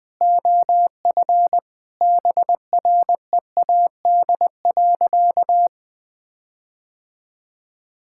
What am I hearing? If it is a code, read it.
Morse code, heard as a keyed tone, about 20 wpm, 700 Hz. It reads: OF BREAD.
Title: Sentences from Top 1600 Words 20wpm